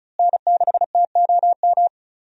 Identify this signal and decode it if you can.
Morse code, heard as a keyed tone, about 34 wpm, 700 Hz. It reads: N6TOM
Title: USA Call Signs - 5 Characters - Repeated 2x - 35wpm